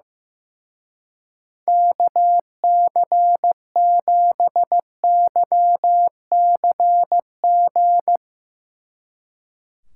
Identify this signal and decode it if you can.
Morse code, heard as a keyed tone, about 16 wpm, 700 Hz. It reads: KC7YCG
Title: Speed Racing - USA 2x3 Call Signs 15wpm